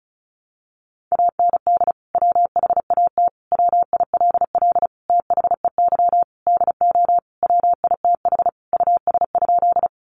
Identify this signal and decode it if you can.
Morse code, heard as a keyed tone, about 34 wpm, 700 Hz. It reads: AND WHAT WILL THEY DO WITH US?
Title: Sentences from Top 200 Words 35wpm